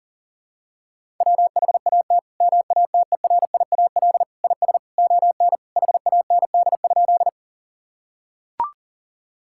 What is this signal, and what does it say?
Morse code, heard as a keyed tone, about 37 wpm, 700 Hz. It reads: WHAT MATERIAL IS ON HAND?
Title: Sentences from Top 1000 Words - Review Part 5 - 40wpm